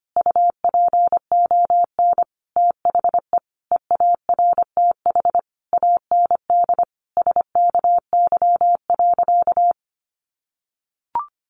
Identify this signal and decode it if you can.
Morse code, heard as a keyed tone, about 25 wpm, 700 Hz. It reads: UPON THE EARTH AND SKY.
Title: Sentences from Top 1200 Words 25wpm